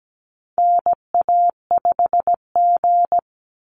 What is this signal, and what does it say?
Morse code, heard as a keyed tone, about 17 wpm, 700 Hz. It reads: NA5G
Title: Call Signs Repeated 3x - 17wpm